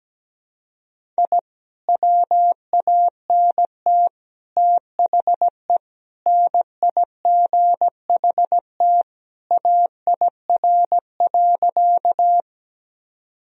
Morse code, heard as a keyed tone, about 17 wpm, 700 Hz. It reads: I WANT THE NIGHT AIR.
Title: Sentences from Top 1000 Words - Review Part 1 - 17wpm